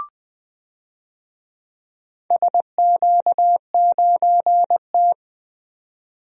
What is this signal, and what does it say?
Morse code, heard as a keyed tone, about 20 wpm, 700 Hz. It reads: SQ9T
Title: DXCC-100 Call Signs - 4 Characters - 20wpm